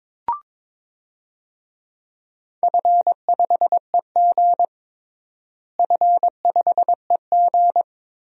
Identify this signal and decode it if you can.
Morse code, heard as a keyed tone, about 22 wpm, 700 Hz. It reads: F5EG F5EG
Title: DXCC-100 Call Signs - 4 Characters - Repeated 2x - 22wpm